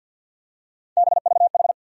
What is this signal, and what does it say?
Morse code, heard as a keyed tone, about 53 wpm, 700 Hz. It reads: BVH
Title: USA Call Sign Postfixes - 3 Letters 50wpm